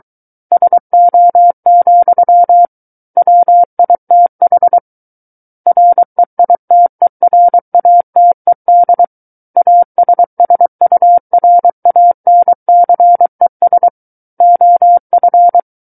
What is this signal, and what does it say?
Morse code, heard as a keyed tone, about 23 wpm, 700 Hz. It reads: SO, WITH REITERATED ASSURANCES OF
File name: Book One / 14-Chapter 14 - 23 WPM